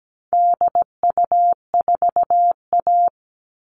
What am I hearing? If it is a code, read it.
Morse code, heard as a keyed tone, about 17 wpm, 700 Hz. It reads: DU4A
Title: All Cut Numbers - 4-Digit 17wpm